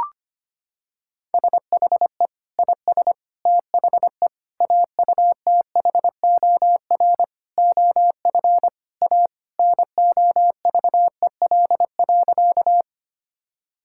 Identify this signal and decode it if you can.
Morse code, heard as a keyed tone, about 25 wpm, 700 Hz. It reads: SHE IS THE AUTHOR OF A NOVEL.
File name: practice / 25wpm / Sentences with 4-Word Phrases - 25wpm